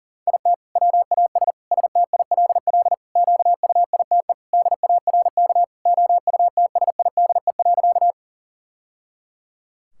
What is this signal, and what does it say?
Morse code, heard as a keyed tone, about 40 wpm, 700 Hz. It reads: IT WAS STILL QUITE DARK OUTSIDE.
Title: Sentences from Top 1000 Words - Review Part 6 - 40wpm